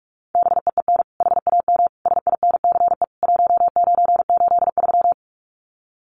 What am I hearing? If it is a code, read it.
Morse code, heard as a keyed tone, about 44 wpm, 700 Hz. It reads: BEEN HAM SINCE 1983
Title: Common QSO Phrases 45wpm